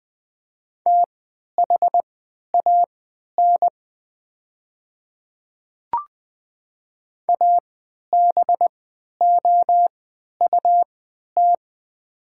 Morse code, heard as a keyed tone, about 21 wpm, spaced as for 12 wpm, 690 Hz. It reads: THAN ABOUT
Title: 3x Character Spacing - Top 500 Words - 20wpm